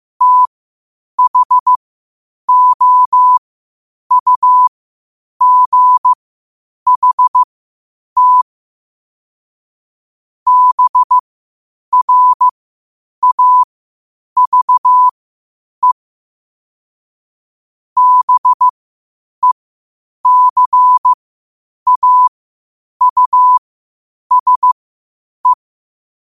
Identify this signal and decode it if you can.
Morse code, heard as a keyed tone, about 15 wpm, spaced as for 8 wpm, 1000 Hz. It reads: THOUGHT BRAVE BECAUSE
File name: qotdfiles / 8wpm